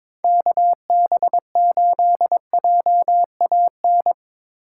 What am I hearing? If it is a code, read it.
Morse code, heard as a keyed tone, about 24 wpm, 700 Hz. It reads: KB8JAN
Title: Speed Racing - USA 2x3 Call Signs 22wpm